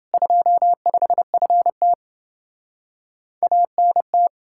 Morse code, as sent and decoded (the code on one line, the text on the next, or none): ..--- ..... ..-. - / .- -. -
25FT ANT